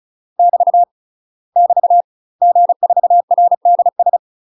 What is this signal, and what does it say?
Morse code, heard as a keyed tone, about 34 wpm, 700 Hz. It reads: <BT> <BT> G4RDS
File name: UK MASTER QSOs TRAINING/35WPM